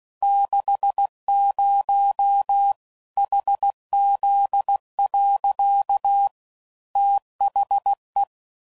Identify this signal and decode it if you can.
Morse code, heard as a keyed tone, about 16 wpm, 800 Hz. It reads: 60 HZ. THE
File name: CW Code Groups @15WPM / Technical Subjects / PwrSupply